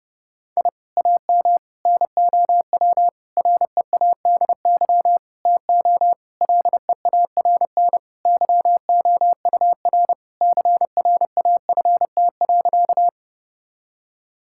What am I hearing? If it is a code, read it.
Morse code, heard as a keyed tone, about 32 wpm, 700 Hz. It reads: I AM NOW READY TO LEARN YOUR CRAFT.
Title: Sentences from Top 1000 Words - Review Part 6 - 30wpm